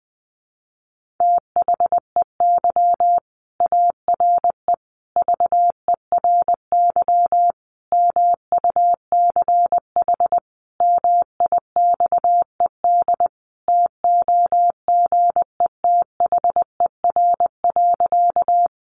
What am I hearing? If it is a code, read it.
Morse code, heard as a keyed tone, about 20 wpm, 700 Hz. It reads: THEY ARE VERY MUCH MIXED TOGETHER.
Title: Sentences from Top 1500 Words 20wpm